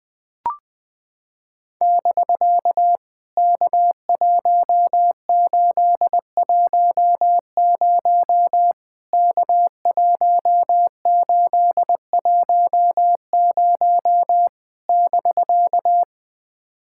Morse code, heard as a keyed tone, about 20 wpm, 700 Hz. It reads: <BK> K1810 K1810 <BK>
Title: POTA QSO Practice - 20wpm